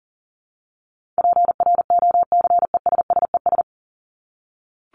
Morse code, heard as a keyed tone, about 40 wpm, 700 Hz. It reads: PROCESSES
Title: ES:S Suffix - Top 100 Words with Base Words 40wpm